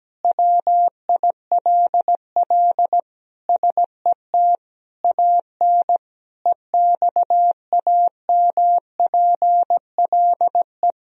Morse code, heard as a keyed tone, about 17 wpm, 700 Hz. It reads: WILL SET AN EXAMPLE
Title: Sentences from Top 1000 Words - Review Part 3 - 17wpm